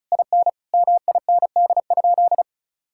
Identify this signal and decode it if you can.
Morse code, heard as a keyed tone, about 36 wpm, 700 Hz. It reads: IN MIND?
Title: Sentences from Top 1000 Words - Review Part 5 - 35wpm